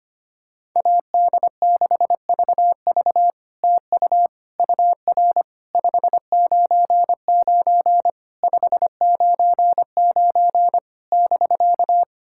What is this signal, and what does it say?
Morse code, heard as a keyed tone, about 25 wpm, 700 Hz. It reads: AD6VV TU UR 599 599 <BK>
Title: SOTA QSO Practice 25wpm